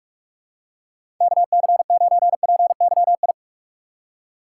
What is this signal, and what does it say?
Morse code, heard as a keyed tone, about 42 wpm, 700 Hz. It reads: KC9PYI
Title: USA Call Signs - 6 Characters - 6x Character Spacing - 45wpm